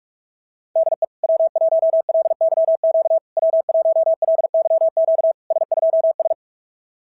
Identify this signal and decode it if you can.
Morse code, heard as a keyed tone, about 46 wpm, 690 Hz. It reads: DE W1LYQ W1LYQ S2S
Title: SOTA QSO Practice 45wpm